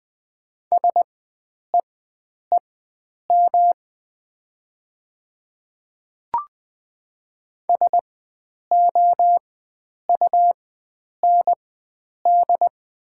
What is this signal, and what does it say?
Morse code, heard as a keyed tone, about 20 wpm, spaced as for 9 wpm, 690 Hz. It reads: SEEM SOUND
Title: 4x Character Spacing - Top 300 Words - 20wpm